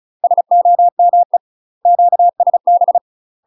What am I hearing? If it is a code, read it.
Morse code, heard as a keyed tone, about 36 wpm, 700 Hz. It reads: SOME QSB
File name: UK MASTER QSOs TRAINING / 35WPM / UK MASTER QSOs 0034